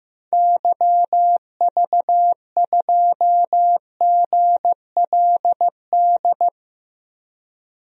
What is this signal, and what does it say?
Morse code, heard as a keyed tone, about 15 wpm, 700 Hz. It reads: YV2GLD
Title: DXCC-100 Call Signs - 6 Characters - Repeated 2x - 15wpm